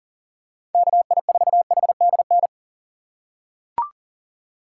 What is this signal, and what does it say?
Morse code, heard as a keyed tone, about 40 wpm, 700 Hz. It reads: KI4HDN
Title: USA Call Signs - 6 Characters - 4x Character Spacing - 40wpm